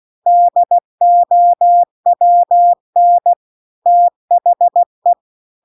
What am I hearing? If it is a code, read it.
Morse code, heard as a keyed tone, about 16 wpm, 700 Hz. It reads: DOWN THE